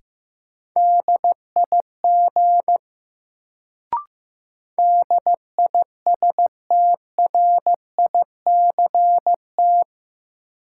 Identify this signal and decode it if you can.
Morse code, heard as a keyed tone, about 16 wpm, 700 Hz. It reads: DIG DISTRICT
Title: Sentences from Top 1600 Words - New Word Drill Alt - 15wpm